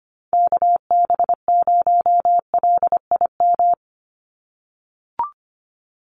Morse code, heard as a keyed tone, about 24 wpm, 700 Hz. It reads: KB0LIM
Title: USA Call Signs 25wpm